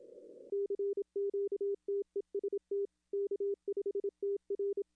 Morse code, heard as a keyed tone, about 27 wpm, 400 Hz. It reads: CQ TEST K5TR
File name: contests / arrl-160-k5tr-op-w5tn -0200